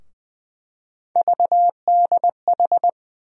Morse code, heard as a keyed tone, about 20 wpm, 700 Hz. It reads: VDH
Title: USA Call Sign Postfixes - 3 Letters 20wpm